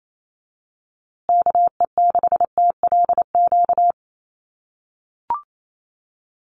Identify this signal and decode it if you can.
Morse code, heard as a keyed tone, about 28 wpm, 700 Hz. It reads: KE6TLQ
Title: USA Call Signs Repeated 2x 28wpm